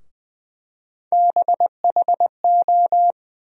Morse code, heard as a keyed tone, about 20 wpm, 700 Hz. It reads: BHO
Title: USA Call Sign Postfixes - 3 Letters 20wpm